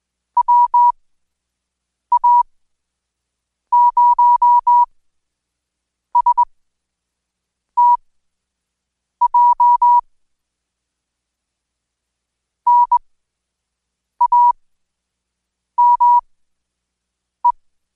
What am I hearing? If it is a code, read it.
Morse code, heard as a keyed tone, about 20 wpm, spaced as for 6 wpm, 970 Hz. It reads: WA0STJ NAME
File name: Code Quick/AudioDownld2/Seg109-Day14-DayDrill 14